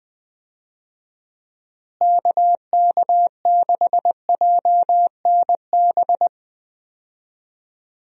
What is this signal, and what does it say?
Morse code, heard as a keyed tone, about 20 wpm, 700 Hz. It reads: KK6JNB
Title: USA Call Signs - 6 Characters - Repeated 3x - 20wpm